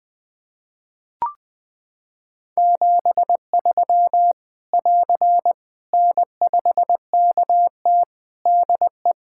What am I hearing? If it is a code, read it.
Morse code, heard as a keyed tone, about 20 wpm, 700 Hz. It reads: 73 <AR> N5KT DE
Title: Practice QSO - Exchange 4 - Ending - 20wpm